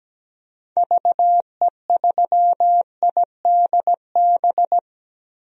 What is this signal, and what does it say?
Morse code, heard as a keyed tone, about 17 wpm, 700 Hz. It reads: VE3IDB
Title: DXCC-100 Call Signs - 6 Characters - Repeated 2x - 17wpm